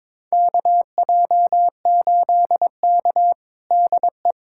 Morse code, heard as a keyed tone, about 21 wpm, 700 Hz. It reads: KJ8K DE
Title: USA Callsign DE Callsign - 4 Characters - 22wpm